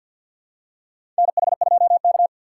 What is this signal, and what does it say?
Morse code, heard as a keyed tone, about 49 wpm, 700 Hz. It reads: NH2K
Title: USA Call Signs - 4 Characters - 6x Character Spacing - 50wpm